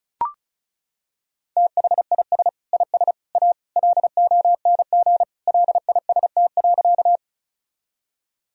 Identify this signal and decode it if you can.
Morse code, heard as a keyed tone, about 36 wpm, 700 Hz. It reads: THIS IS A LONG LIST.
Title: Sentences from Top 1000 Words - Review Part 2 - 35wpm